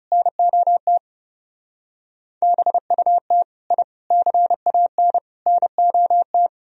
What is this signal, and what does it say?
Morse code, heard as a keyed tone, about 37 wpm, 700 Hz. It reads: NOT BUT I CAN NOT
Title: Speed-Racing - Sets of 4 Words 30wpm